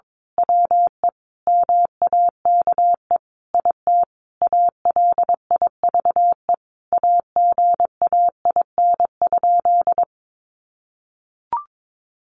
Morse code, heard as a keyed tone, about 22 wpm, 700 Hz. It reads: WE MAKE IT ALIVE AGAIN?
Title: A Sound Of Thunder 22wpm